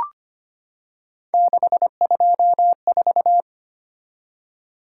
Morse code, heard as a keyed tone, about 25 wpm, 700 Hz. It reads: 624
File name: Three Letter-Number Combinations - 25wpm